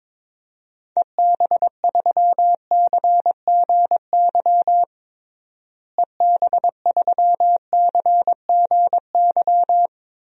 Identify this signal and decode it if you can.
Morse code, heard as a keyed tone, about 22 wpm, 700 Hz. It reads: EB3CGY EB3CGY
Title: DXCC-100 Call Signs - 6 Characters - Repeated 3x - 22wpm